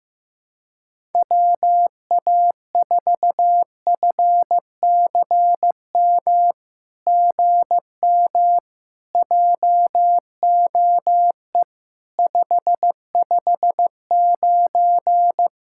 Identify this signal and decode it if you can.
Morse code, heard as a keyed tone, about 15 wpm, 700 Hz. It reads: WA4FCM GM JOE 559
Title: SOTA QSO Practice 15wpm